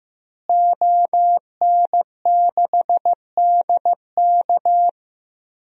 Morse code, heard as a keyed tone, about 16 wpm, 700 Hz. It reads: ON6DK
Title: DXCC-100 Call Signs - 5 Characters - Repeated 3x - 15wpm